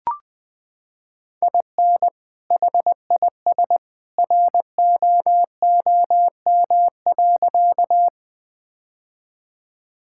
Morse code, heard as a keyed tone, about 20 wpm, 700 Hz. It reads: IN HIS ROOM.
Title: Sentences with 2-Word Phrases - 20wpm